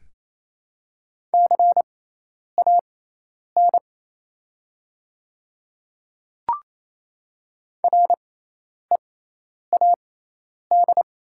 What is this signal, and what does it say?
Morse code, heard as a keyed tone, about 28 wpm, spaced as for 10 wpm, 700 Hz. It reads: CAN READ
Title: 6x Character Spacing - Top 300 Words - 28wpm